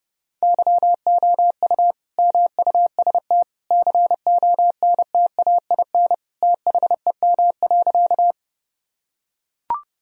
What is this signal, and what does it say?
Morse code, heard as a keyed tone, about 30 wpm, 700 Hz. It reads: YOU MUST CONTAIN THEM.
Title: Sentences from Top 1000 Words - Review Part 2 - 30wpm